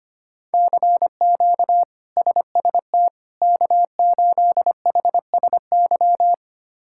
Morse code, heard as a keyed tone, about 25 wpm, 700 Hz. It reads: CQ SST K8HSY
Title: K1USN SST QSO Practice - 25wpm